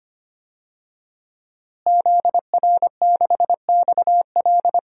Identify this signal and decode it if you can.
Morse code, heard as a keyed tone, about 25 wpm, 700 Hz. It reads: ZR6XL
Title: DXCC-100 Call Signs - 5 Characters - Repeated 2x - 25wpm